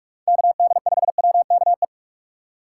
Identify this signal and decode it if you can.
Morse code, heard as a keyed tone, about 44 wpm, 700 Hz. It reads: KD5WKE
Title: USA Call Signs - 6 Characters - 2x Character Spacing - 45wpm